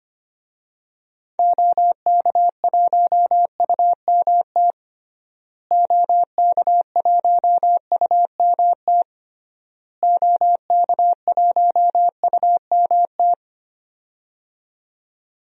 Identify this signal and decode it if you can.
Morse code, heard as a keyed tone, about 25 wpm, 700 Hz. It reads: OK1UMT OK1UMT OK1UMT
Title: DXCC-100 Call Signs - 6 Characters - Repeated 3x - 25wpm